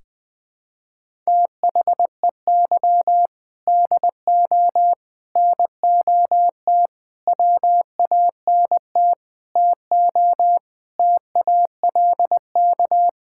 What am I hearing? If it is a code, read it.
Morse code, heard as a keyed tone, about 20 wpm, 700 Hz. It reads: THEY DO NOT WANT TO TALK
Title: Sentences with 5-Word Phrases - 20wpm